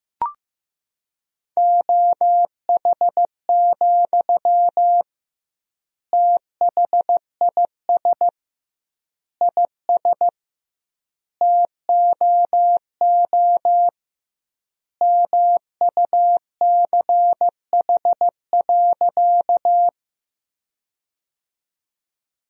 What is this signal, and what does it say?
Morse code, heard as a keyed tone, about 15 wpm, 700 Hz. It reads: OH, THIS IS TOO MUCH.
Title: 2x Word-Spacing - Sentences from Top 500 Words - 15wpm